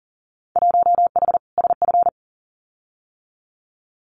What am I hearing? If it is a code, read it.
Morse code, heard as a keyed tone, about 42 wpm, 700 Hz. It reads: 1H SF
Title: Winter Field Day Exchanges 40wpm